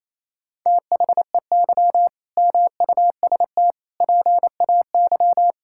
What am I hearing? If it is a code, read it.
Morse code, heard as a keyed tone, about 28 wpm, 700 Hz. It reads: THEY MUST PAY
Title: Sentences from Top 1000 Words - Review Part 6 - 28wpm